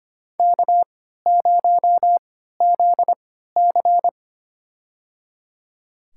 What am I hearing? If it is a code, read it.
Morse code, heard as a keyed tone, about 25 wpm, spaced as for 14 wpm, 700 Hz. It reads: K0ZC
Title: USA Call Signs - 3x Character Spacing - 25wpm